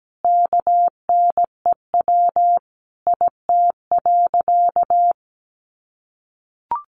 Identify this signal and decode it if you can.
Morse code, heard as a keyed tone, about 17 wpm, 700 Hz. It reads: KNEW IT.
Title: Sentences from Top 1400 Words 17wpm